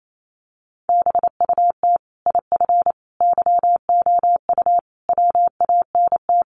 Morse code, heard as a keyed tone, about 28 wpm, 700 Hz. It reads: BUT IF YOU WANT
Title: Sets of 5 Words Encore2 28wpm